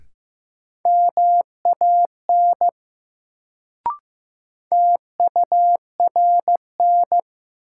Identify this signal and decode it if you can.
Morse code, heard as a keyed tone, about 15 wpm, 700 Hz. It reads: MAN TURN
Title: Sentences from Top 200 Words - New Word Drill - 15wpm